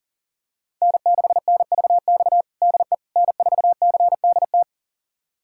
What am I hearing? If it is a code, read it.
Morse code, heard as a keyed tone, about 37 wpm, 700 Hz. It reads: N6NVX DE N4CDT
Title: USA Callsign DE Callsign - 5 Characters - 40wpm